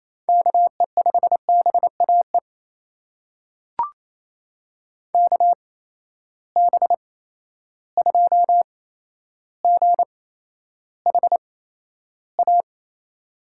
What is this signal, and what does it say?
Morse code, heard as a keyed tone, about 28 wpm, spaced as for 12 wpm, 700 Hz. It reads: KE5BAE KB2GHA
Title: USA Call Signs - 6 Characters - 8x Character Spacing - 28wpm